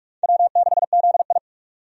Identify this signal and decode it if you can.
Morse code, heard as a keyed tone, about 46 wpm, 700 Hz. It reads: W6ZI
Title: USA Call Signs - 4 Characters - 45wpm